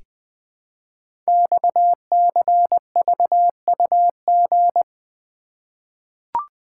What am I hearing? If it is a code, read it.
Morse code, heard as a keyed tone, about 20 wpm, 700 Hz. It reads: XCVUG
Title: Code Groups - 5 Letter:Numbers - 20wpm